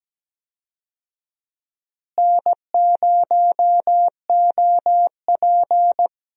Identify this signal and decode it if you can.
Morse code, heard as a keyed tone, about 17 wpm, 700 Hz. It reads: N0OP
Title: USA Call Signs - 4 Characters - Repeated 2x - 17wpm